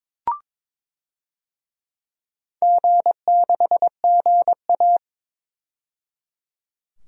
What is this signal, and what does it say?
Morse code, heard as a keyed tone, about 22 wpm, 690 Hz. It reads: G6GA
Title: DXCC-100 Call Signs - 4 Characters - 22wpm